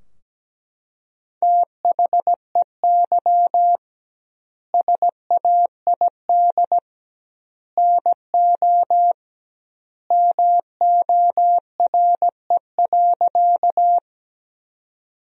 Morse code, heard as a keyed tone, about 17 wpm, 700 Hz. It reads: THEY SAID NO MORE.
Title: 2x Word-Spacing - Sentences from Top 100 Words - 17wpm